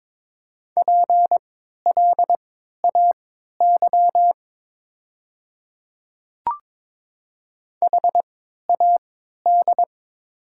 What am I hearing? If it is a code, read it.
Morse code, heard as a keyed tone, about 22 wpm, spaced as for 13 wpm, 690 Hz. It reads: PLAY HAD
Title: 3x Character Spacing - Top 200 Words - 22wpm